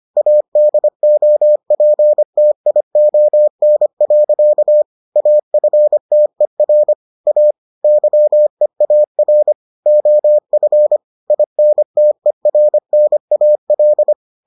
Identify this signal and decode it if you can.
Morse code, heard as a keyed tone, about 25 wpm, 600 Hz. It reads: ADOPTION. AFTER A YEAR OF INTERNAL